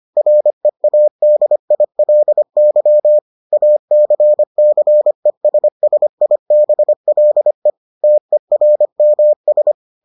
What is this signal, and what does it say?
Morse code, heard as a keyed tone, about 25 wpm, 600 Hz. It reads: READILY ACCESSIBLE TERMS